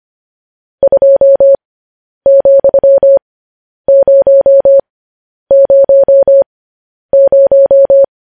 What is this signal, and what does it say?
Morse code, heard as a keyed tone, about 25 wpm, spaced as for 10 wpm, 550 Hz. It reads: 2,000
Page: news-headlines-in-morse-code-at-10-wpm